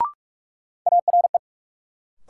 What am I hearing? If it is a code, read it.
Morse code, heard as a keyed tone, about 46 wpm, 700 Hz. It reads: ARE
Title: Sentences from Top 100 Words - New Word Drill - 45wpm